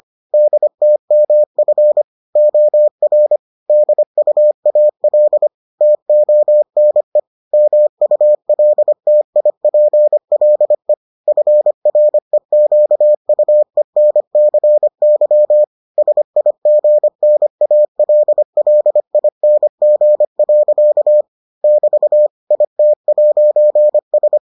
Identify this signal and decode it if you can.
Morse code, heard as a keyed tone, about 25 wpm, 600 Hz. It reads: DTMF OR DUAL TONE MULTIPLE FREQUENCY SIGNALLING. <BT> IT'S